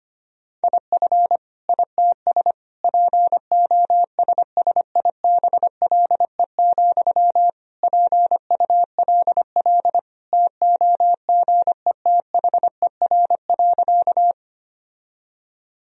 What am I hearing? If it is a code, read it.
Morse code, heard as a keyed tone, about 25 wpm, 700 Hz. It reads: IF ITS POSSIBLE, PULL TOGETHER.
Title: Sentences from Top 1000 Words - Review Part 2 - 25wpm